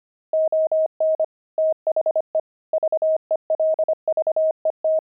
Morse code, heard as a keyed tone, about 25 wpm, 630 Hz. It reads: ON THE VELVET
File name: pangram-fs-25-cs-25-fragment